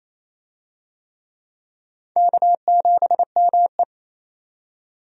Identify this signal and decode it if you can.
Morse code, heard as a keyed tone, about 27 wpm, 700 Hz. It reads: K7ME
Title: USA Call Signs - 4 Characters - 28wpm